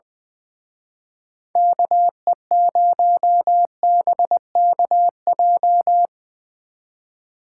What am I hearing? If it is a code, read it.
Morse code, heard as a keyed tone, about 22 wpm, 700 Hz. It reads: KE0BKJ
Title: Speed Racing - USA 2x3 Call Signs 20wpm